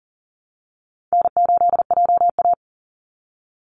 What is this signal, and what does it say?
Morse code, heard as a keyed tone, about 40 wpm, 700 Hz. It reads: N8JA
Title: Easy Call Signs Repeated 3x - 40wpm